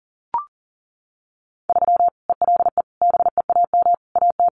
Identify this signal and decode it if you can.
Morse code, heard as a keyed tone, about 40 wpm, 700 Hz. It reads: 3 ELE BEAM AT
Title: Common QSO Phrases 40wpm